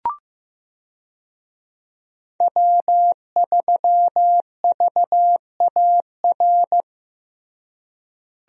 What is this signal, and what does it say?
Morse code, heard as a keyed tone, about 15 wpm, 700 Hz. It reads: W3VAR
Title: USA Call Signs - 5 Characters - Repeated 3x - 15wpm